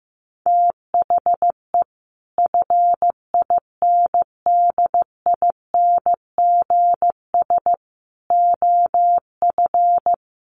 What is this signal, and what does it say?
Morse code, heard as a keyed tone, about 15 wpm, 700 Hz. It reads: THE FINDINGS OF
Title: Sets of 5 Words Encore2 15wpm